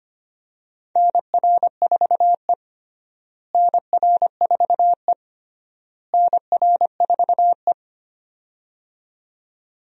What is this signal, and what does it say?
Morse code, heard as a keyed tone, about 25 wpm, 700 Hz. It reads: NR4E NR4E NR4E
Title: USA Call Signs - 4 Characters - Repeated 3x - 25wpm